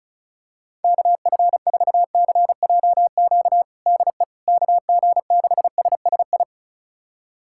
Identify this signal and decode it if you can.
Morse code, heard as a keyed tone, about 36 wpm, 700 Hz. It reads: KF4CJQ DE KG6SSI
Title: USA Callsign DE Callsign - 6 Characters - 35wpm